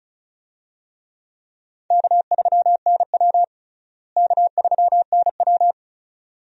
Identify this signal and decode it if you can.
Morse code, heard as a keyed tone, about 34 wpm, 700 Hz. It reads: K3NW K3NW
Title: USA Call Signs - 4 Characters - Repeated 2x - 35wpm